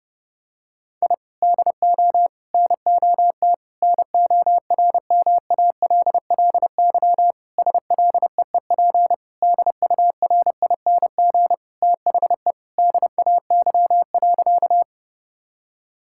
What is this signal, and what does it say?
Morse code, heard as a keyed tone, about 28 wpm, 700 Hz. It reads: I DO NOT NORMALLY SLEEP DURING THE DAY.
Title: Sentences from Top 1000 Words - Review Part 6 - 30wpm